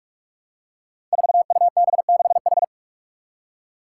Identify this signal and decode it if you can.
Morse code, heard as a keyed tone, about 41 wpm, 700 Hz. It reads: 4UB6H
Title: Five Challenging Letter-Number Combinations 45wpm